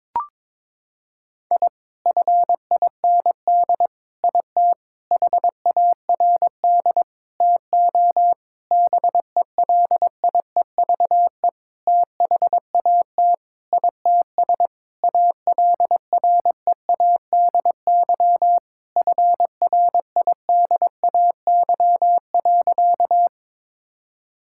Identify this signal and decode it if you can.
Morse code, heard as a keyed tone, about 22 wpm, 700 Hz. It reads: I FIND IT HARD TO BELIEVE THAT ITS ALREADY FRIDAY.
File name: Sentences with 7-Word Phrases - 22wpm